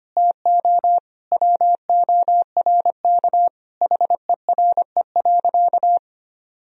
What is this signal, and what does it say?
Morse code, heard as a keyed tone, about 25 wpm, 700 Hz. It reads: TO WORK HERE.
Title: Sentences from Top 1000 Words - Review Part 4 - 25wpm